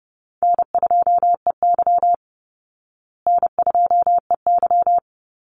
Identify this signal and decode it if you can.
Morse code, heard as a keyed tone, about 30 wpm, 700 Hz. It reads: N2EY N2EY
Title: Call Signs Repeated 2x - 30wpm